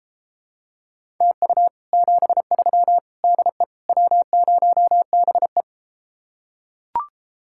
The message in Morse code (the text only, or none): TU 73 DE W0BE